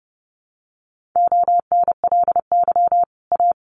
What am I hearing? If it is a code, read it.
Morse code, heard as a keyed tone, about 30 wpm, 700 Hz. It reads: ONLY A
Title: Sets of 5 Words Encore 30wpm